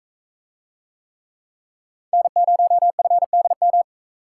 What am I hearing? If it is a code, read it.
Morse code, heard as a keyed tone, about 38 wpm, 700 Hz. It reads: N0FDM